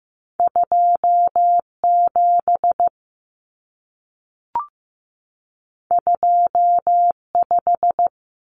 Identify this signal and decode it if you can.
Morse code, heard as a keyed tone, about 15 wpm, 700 Hz. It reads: 27 25
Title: Common Cut Numbers - 2-Digit 15wpm